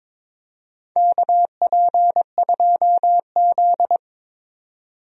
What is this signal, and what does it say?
Morse code, heard as a keyed tone, about 22 wpm, 700 Hz. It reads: KP2Z
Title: DXCC-100 Call Signs - 4 Characters - 22wpm